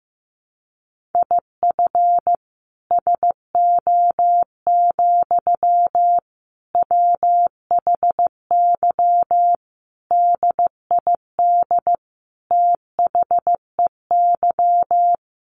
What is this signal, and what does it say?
Morse code, heard as a keyed tone, about 15 wpm, 700 Hz. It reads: IF SO, WHY DID THEY
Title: Sentences from Top 1400 Words 15wpm